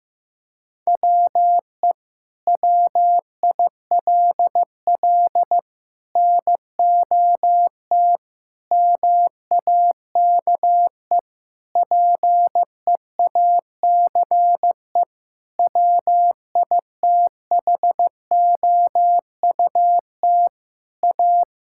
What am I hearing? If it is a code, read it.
Morse code, heard as a keyed tone, about 15 wpm, 700 Hz. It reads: WE WILL NOT MAKE PEACE WITHOUT A
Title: Sentences from Top 1600 Words 15wpm